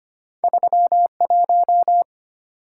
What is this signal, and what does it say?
Morse code, heard as a keyed tone, about 25 wpm, 700 Hz. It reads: 31
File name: Two Letter-Number Combinations - 25wpm